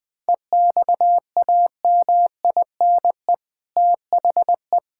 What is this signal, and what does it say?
Morse code, heard as a keyed tone, about 19 wpm, 700 Hz. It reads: EXAMINE THE
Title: Sentences from Top 1600 Words 20wpm